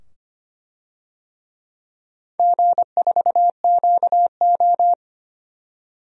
Morse code, heard as a keyed tone, about 25 wpm, 700 Hz. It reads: G4QO
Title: DXCC-100 Call Signs - 4 Characters - 25wpm